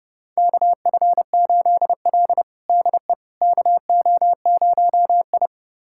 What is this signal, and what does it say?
Morse code, heard as a keyed tone, about 30 wpm, 700 Hz. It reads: KF8L DE KO0I
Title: USA Callsign DE Callsign - 4 Characters - 30wpm